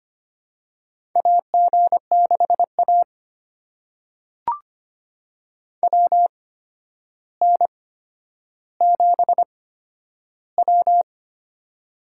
Morse code, heard as a keyed tone, about 25 wpm, spaced as for 11 wpm, 700 Hz. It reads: AG6A WN7W
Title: USA Call Signs - 4 Characters - 8x Character Spacing - 25wpm